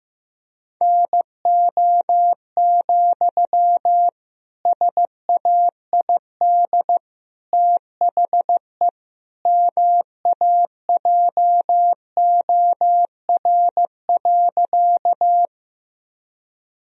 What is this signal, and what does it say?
Morse code, heard as a keyed tone, about 15 wpm, 700 Hz. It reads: NO, SAID THE MAJOR.
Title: Sentences from Top 1000 Words - Review Part 4 - 15wpm